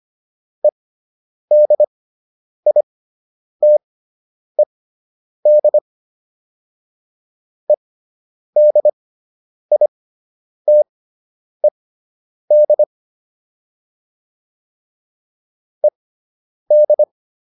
Morse code, heard as a keyed tone, about 25 wpm, spaced as for 9 wpm, 600 Hz. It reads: EDITED EDITED ED